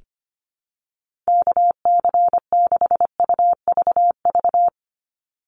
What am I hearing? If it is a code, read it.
Morse code, heard as a keyed tone, about 25 wpm, 700 Hz. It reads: KC6UVV
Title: USA Call Signs 25wpm